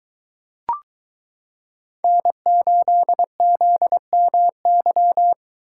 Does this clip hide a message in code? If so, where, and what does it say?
Yes: Morse code from 2.0 to 5.3 s, heard as a keyed tone, about 22 wpm, 700 Hz. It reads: N8ZMY